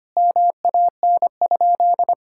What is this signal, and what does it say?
Morse code, heard as a keyed tone, about 25 wpm, 700 Hz. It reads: MAN?
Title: Sentences from Top 1000 Words - Review Part 5 - 25wpm